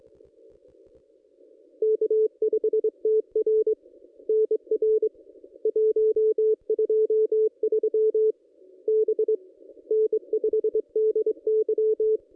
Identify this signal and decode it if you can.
Morse code, heard as a keyed tone, about 25 wpm, 440 Hz. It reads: K5TR NR 123 B N5DY
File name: contests / ss-cw-k5tr-op-n5rz -1900